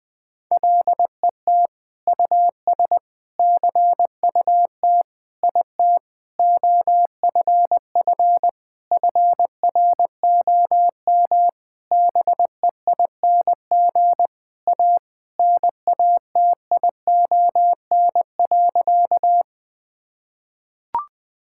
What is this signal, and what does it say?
Morse code, heard as a keyed tone, about 20 wpm, 700 Hz. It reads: LET US CUT IT OFF FROM BEING A NATION.
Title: Sentences from Top 1000 Words - Review Part 6 - 20wpm